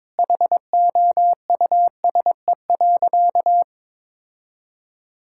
Morse code, heard as a keyed tone, about 22 wpm, 700 Hz. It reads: HOUSE.
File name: practice / 22wpm / Sentences with 2-Word Phrases - 22wpm